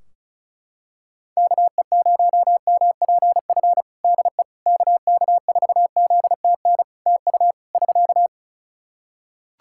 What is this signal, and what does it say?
Morse code, heard as a keyed tone, about 36 wpm, 700 Hz. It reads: KE0MPF DE KK4ZTN TU <SK>
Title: Practice QSO - Exchange 4 - Ending - 35wpm